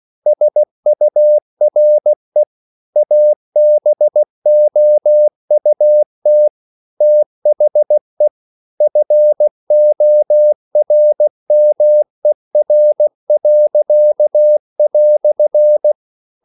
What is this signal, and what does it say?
Morse code, heard as a keyed tone, about 16 wpm, 600 Hz. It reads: SURE ABOUT THE FORMER."
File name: CW / Practice / EnglishQuotations0116